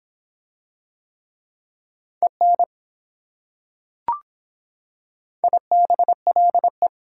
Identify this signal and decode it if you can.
Morse code, heard as a keyed tone, about 24 wpm, 700 Hz. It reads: EN IBLE